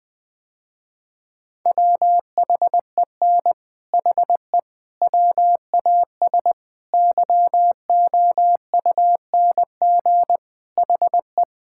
Morse code, heard as a keyed tone, about 20 wpm, 700 Hz. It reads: WHEN HE WAS YOUNG HE
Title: Sentences with 3-Word Phrases - Spoken Beginning - 20wpm